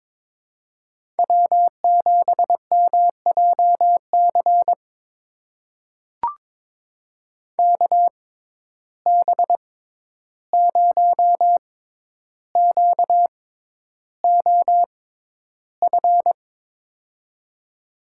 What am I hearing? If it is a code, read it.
Morse code, heard as a keyed tone, about 22 wpm, spaced as for 11 wpm, 700 Hz. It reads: W7MJC KB0QOF
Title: USA Call Signs - 6x Character Spacing - 22wpm